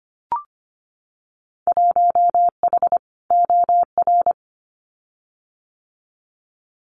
Morse code, heard as a keyed tone, about 25 wpm, 700 Hz. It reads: 1H OR
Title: Winter Field Day Exchanges 25wpm